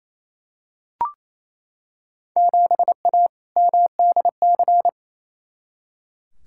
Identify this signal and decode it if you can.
Morse code, heard as a keyed tone, about 28 wpm, 700 Hz. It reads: 7A MDC
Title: Field Day Exchanges Encore - 28wpm